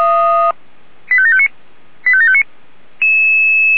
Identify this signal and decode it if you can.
Five-tone selective call (ZVEI), heard as quick runs of five tones, tones of about 70 ms. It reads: 85759 85759